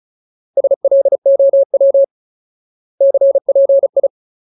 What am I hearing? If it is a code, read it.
Morse code, heard as a keyed tone, about 36 wpm, 550 Hz. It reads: SLOW CPI
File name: Ultimate Word List / 35WPM Ultimate